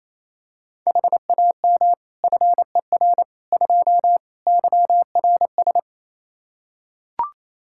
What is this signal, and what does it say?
Morse code, heard as a keyed tone, about 27 wpm, 700 Hz. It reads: HAM FER 2 YRS
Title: Practice QSO - Exchange 3 - More Info - 28wpm